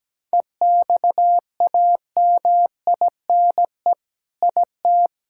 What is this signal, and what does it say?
Morse code, heard as a keyed tone, about 17 wpm, 700 Hz. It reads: EXAMINE IT
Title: Sentences from Top 1600 Words 17wpm